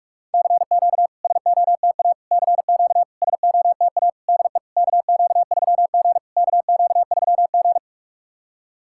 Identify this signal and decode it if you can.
Morse code, heard as a keyed tone, about 46 wpm, 690 Hz. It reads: CQ SOTA CQ SOTA DE KQ3G KQ3G
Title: SOTA QSO Practice 45wpm